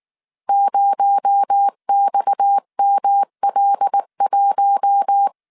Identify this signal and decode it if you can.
Morse code, heard as a keyed tone, about 19 wpm, 800 Hz. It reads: 0XML1
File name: CW / DET / 1 6 1500C S C 18wpm 2356